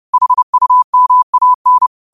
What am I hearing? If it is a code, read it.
Morse code, heard as a keyed tone, about 30 wpm, 1000 Hz. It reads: HUMAN
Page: 30 WPM Morse code quotes